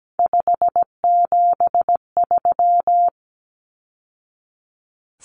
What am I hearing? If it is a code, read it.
Morse code, heard as a keyed tone, about 17 wpm, 700 Hz. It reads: 573
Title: Numbers - 3-Digit 17wpm